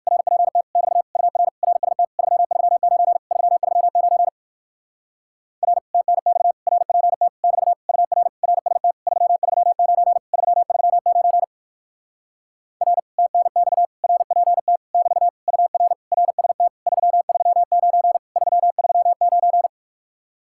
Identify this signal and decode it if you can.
Morse code, heard as a keyed tone, about 59 wpm, 700 Hz. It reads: RPT <BT> UR RST 339 339 R TNX RPT <BT> UR RST 339 339 R TNX RPT <BT> UR RST 339 339